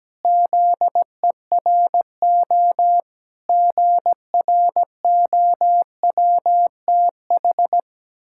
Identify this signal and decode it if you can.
Morse code, heard as a keyed tone, about 17 wpm, 700 Hz. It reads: ZERO GROWTH
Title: Pangram Challenge v2 - 17wpm